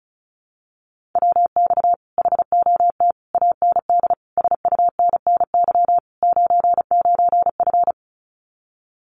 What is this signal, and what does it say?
Morse code, heard as a keyed tone, about 34 wpm, 700 Hz. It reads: WX HOT AND SUNNY 99F
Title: Common QSO Phrases Encore 35wpm